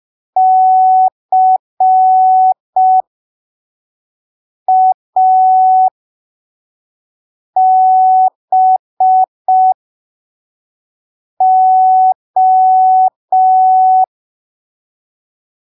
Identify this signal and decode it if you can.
Morse code, heard as a keyed tone, about 5 wpm, spaced as for 3 wpm, 750 Hz. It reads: CABO